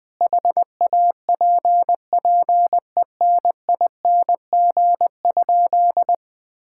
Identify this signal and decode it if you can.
Morse code, heard as a keyed tone, about 20 wpm, 700 Hz. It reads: HAPPENING?
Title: Sentences with 2-Word Phrases - 20wpm